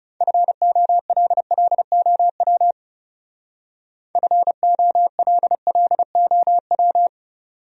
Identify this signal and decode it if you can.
Morse code, heard as a keyed tone, about 38 wpm, 700 Hz. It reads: FOLLOW FOLLOW
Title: Speed-Racing Top 200 Words 30wpm